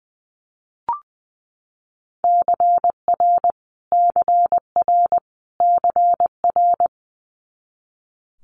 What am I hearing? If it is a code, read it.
Morse code, heard as a keyed tone, about 20 wpm, 700 Hz. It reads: CR CR CR
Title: Multi-letter Phonemes Repeated 3x 20wpm